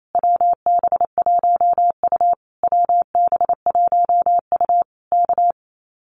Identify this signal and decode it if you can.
Morse code, heard as a keyed tone, about 28 wpm, 700 Hz. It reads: WB1U WB1U K
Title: Common QSO Phrases Encore 28wpm